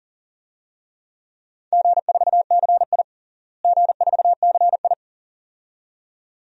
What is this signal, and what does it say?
Morse code, heard as a keyed tone, about 40 wpm, 700 Hz. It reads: G4CI G4CI
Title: DXCC-100 Call Signs - 4 Characters - Repeated 2x - 40wpm